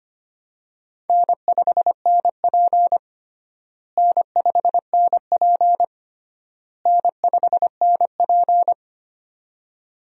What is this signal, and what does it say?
Morse code, heard as a keyed tone, about 25 wpm, 700 Hz. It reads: N5NP N5NP N5NP
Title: USA Call Signs - 4 Characters - Repeated 3x - 25wpm